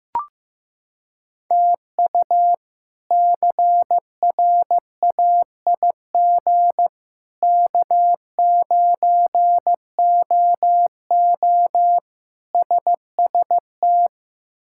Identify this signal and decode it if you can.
Morse code, heard as a keyed tone, about 15 wpm, 700 Hz. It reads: TU CRAIG K9OO SST
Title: K1USN SST QSO Practice - 15wpm